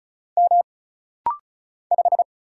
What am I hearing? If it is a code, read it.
Morse code, heard as a keyed tone, about 36 wpm, 700 Hz. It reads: M 5
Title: Single Letter-Number - Rapid-Fire 35wpm